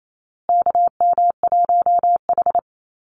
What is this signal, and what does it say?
Morse code, heard as a keyed tone, about 28 wpm, 700 Hz. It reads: KM1H
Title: USA Call Signs Repeated 3x 28wpm